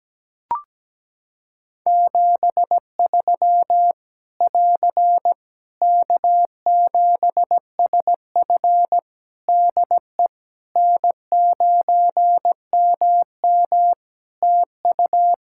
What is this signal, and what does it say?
Morse code, heard as a keyed tone, about 17 wpm, 700 Hz. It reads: 73 <AR> K7SF DE N9MM TU
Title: K1USN SST QSO Practice 17wpm